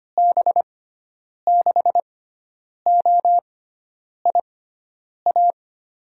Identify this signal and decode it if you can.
Morse code, heard as a keyed tone, about 25 wpm, spaced as for 9 wpm, 700 Hz. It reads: B6OIA
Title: USA Call Signs - 6 Characters - 6x Character Spacing - 25wpm